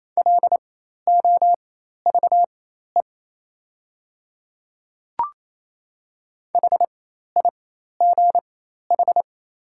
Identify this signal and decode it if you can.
Morse code, heard as a keyed tone, about 27 wpm, spaced as for 13 wpm, 700 Hz. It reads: LOVE HIGH
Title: 4x Character Spacing - Top 500 Words - 28wpm